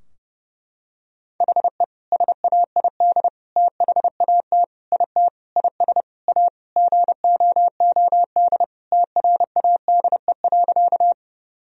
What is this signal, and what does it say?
Morse code, heard as a keyed tone, about 30 wpm, 700 Hz. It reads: HE SAID THAT IT IS A GOOD TRADE.
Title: Sentences from Top 1000 Words - Review Part 4 - 30wpm